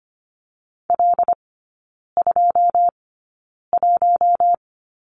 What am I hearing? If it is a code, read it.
Morse code, heard as a keyed tone, about 25 wpm, spaced as for 9 wpm, 690 Hz. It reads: L21
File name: Lesson 14e - Number 2 - Three Characters 25wpm